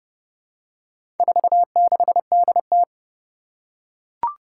Morse code, heard as a keyed tone, about 31 wpm, 700 Hz. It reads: 46DT
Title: Four Challenging Letter-Number Combinations 30wpm